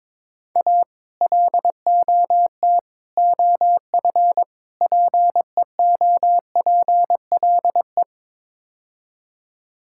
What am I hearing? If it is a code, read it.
Morse code, heard as a keyed tone, about 22 wpm, 700 Hz. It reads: A LOT OF PEOPLE
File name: practice / 22wpm / Build A Phrase - 5-Word Phrases - 22wpm